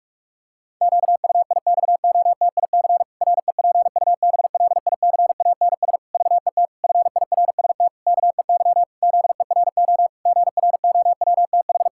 Morse code, heard as a keyed tone, about 46 wpm, 700 Hz. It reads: QUIXOTIC REPUBLICANS VET FIRST KEY ZERO GROWTH
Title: Pangram Challenge v2 - 45wpm